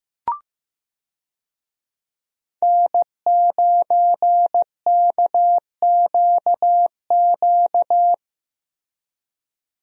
Morse code, heard as a keyed tone, about 16 wpm, 700 Hz. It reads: N9KQQ
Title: USA Call Signs - 5 Characters - Repeated 2x - 15wpm